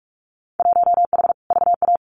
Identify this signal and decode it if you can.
Morse code, heard as a keyed tone, about 44 wpm, 700 Hz. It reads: 1H VA
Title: Winter Field Day Exchanges 45wpm